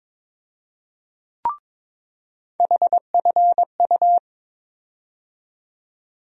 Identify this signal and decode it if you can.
Morse code, heard as a keyed tone, about 21 wpm, 700 Hz. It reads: HFU
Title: USA Call Sign Postfixes - 3 Letters 22wpm